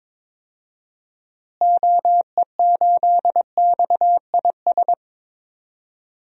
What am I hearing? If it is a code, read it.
Morse code, heard as a keyed tone, about 22 wpm, 700 Hz. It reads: OE8XIS
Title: DXCC-100 Call Signs - 6 Characters - Repeated 3x - 22wpm